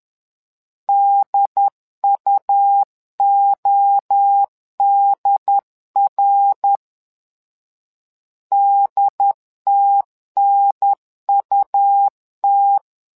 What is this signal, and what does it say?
Morse code, heard as a keyed tone, about 11 wpm, 800 Hz. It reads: DUODR DTNUT
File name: LEZIONI / 3 MWRO / 3 MWRO 10wpm 2148